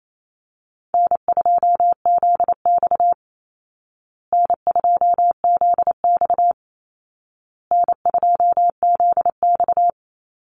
Morse code, heard as a keyed tone, about 28 wpm, 700 Hz. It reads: N2ZX N2ZX N2ZX
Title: Easy Call Signs Repeated 3x - 28wpm